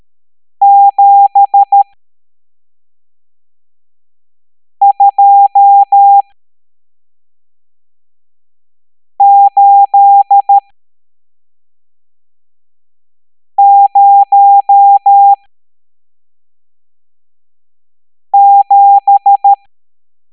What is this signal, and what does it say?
Morse code, heard as a keyed tone, about 13 wpm, spaced as for 3 wpm, 800 Hz. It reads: 72807
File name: hamdata / L12-EX02-CHIFFRES